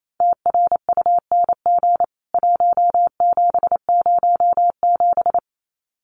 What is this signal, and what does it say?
Morse code, heard as a keyed tone, about 28 wpm, 700 Hz. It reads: TRUNG 1707
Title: CWT Exchange 28wpm